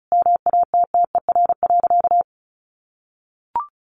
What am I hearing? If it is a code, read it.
Morse code, heard as a keyed tone, about 34 wpm, 700 Hz. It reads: MATTER.
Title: Sentences from Top 1200 Words 35wpm